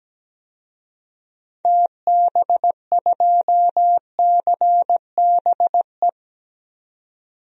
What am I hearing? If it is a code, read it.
Morse code, heard as a keyed tone, about 17 wpm, 700 Hz. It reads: TB2CBE
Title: DXCC-100 Call Signs - 6 Characters - 17wpm